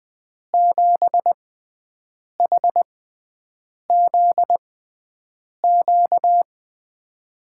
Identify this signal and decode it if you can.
Morse code, heard as a keyed tone, about 20 wpm, spaced as for 7 wpm, 700 Hz. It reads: 7HZQ
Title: USA Call Signs - 5 Characters - 6x Character Spacing - 20wpm